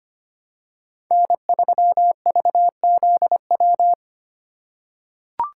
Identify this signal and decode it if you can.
Morse code, heard as a keyed tone, about 25 wpm, 700 Hz. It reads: N3VZW
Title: USA Call Signs - 5 Characters - 2x Character Spacing - 25wpm